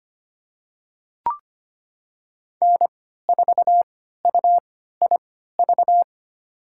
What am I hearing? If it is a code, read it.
Morse code, heard as a keyed tone, about 25 wpm, spaced as for 14 wpm, 700 Hz. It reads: N4UIV
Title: USA Call Signs - 3x Character Spacing - 25wpm